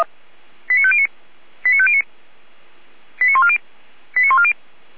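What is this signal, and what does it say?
Five-tone selective call (ZVEI), heard as quick runs of five tones, tones of about 70 ms. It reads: 89509 89509 89140 89140
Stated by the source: ZVEI five-tone selective call received via SDR.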